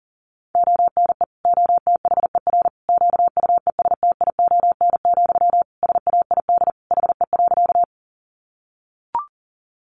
Morse code, heard as a keyed tone, about 40 wpm, 700 Hz. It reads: ONE OTHER QUESTION, SAID HE.
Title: Sentences from Top 400 Words 40wpm